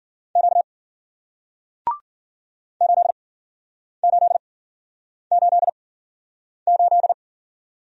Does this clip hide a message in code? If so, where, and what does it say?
Yes: Morse code from 0.4 to 7.1 s, heard as a keyed tone, about 53 wpm, spaced as for 9 wpm, 700 Hz. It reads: Q 8888